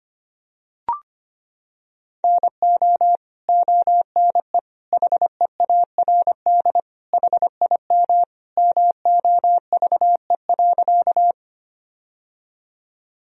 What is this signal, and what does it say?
Morse code, heard as a keyed tone, about 25 wpm, 700 Hz. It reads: NO ONE HEARD HIM MOVE.
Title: Sentences from Top 1000 Words - Review Part 1 - 25wpm